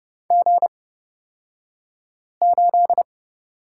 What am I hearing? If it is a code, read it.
Morse code, heard as a keyed tone, about 30 wpm, spaced as for 5 wpm, 700 Hz. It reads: G8
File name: Lesson 37d - Letter x - Two Characters 30wpm